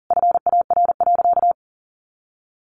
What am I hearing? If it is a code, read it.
Morse code, heard as a keyed tone, about 40 wpm, 700 Hz. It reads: FAR.
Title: Sentences from Top 400 Words 40wpm